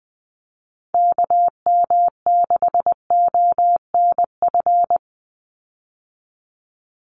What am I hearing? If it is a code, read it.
Morse code, heard as a keyed tone, about 20 wpm, 700 Hz. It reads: KM6ONF
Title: USA Call Signs 20wpm